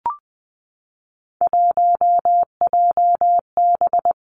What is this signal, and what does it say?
Morse code, heard as a keyed tone, about 20 wpm, 700 Hz. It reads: 1JB
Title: Three Challenging Letter-Number Combinations 20wpm